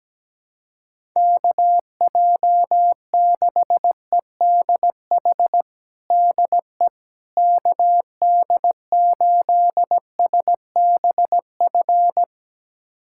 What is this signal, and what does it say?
Morse code, heard as a keyed tone, about 17 wpm, 700 Hz. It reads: KJ6EDH DE KD8SBF
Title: USA Callsign DE Callsign - 6 Characters - 17wpm